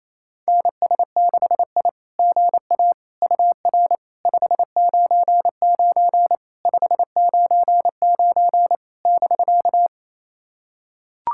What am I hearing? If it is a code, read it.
Morse code, heard as a keyed tone, about 28 wpm, 700 Hz. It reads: NS6I GA UR 599 599 <BK>
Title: SOTA QSO Practice 28wpm